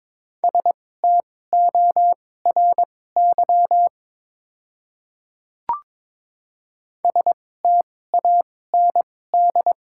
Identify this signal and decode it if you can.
Morse code, heard as a keyed tone, about 22 wpm, spaced as for 16 wpm, 690 Hz. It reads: STORY STAND
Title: 2x Character Spacing - Top 500 Words - 22wpm